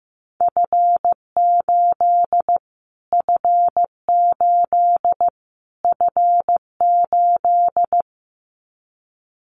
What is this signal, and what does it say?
Morse code, heard as a keyed tone, about 15 wpm, 700 Hz. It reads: F8 F8 F8
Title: Callsign Prefixes for Top 25 DXCC - Repeated 3x - 15wpm